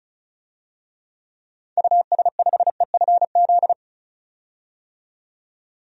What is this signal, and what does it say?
Morse code, heard as a keyed tone, about 36 wpm, 700 Hz. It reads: US5EFZ
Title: DXCC-100 Call Signs - 6 Characters - 35wpm